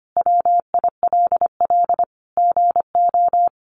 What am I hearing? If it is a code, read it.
Morse code, heard as a keyed tone, about 25 wpm, 700 Hz. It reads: WILL GO
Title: Sentences from Top 400 Words 25wpm